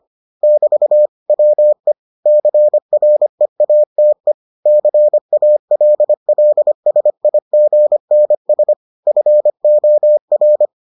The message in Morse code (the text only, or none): <BT> WE CREATE CALLSIGNS FOR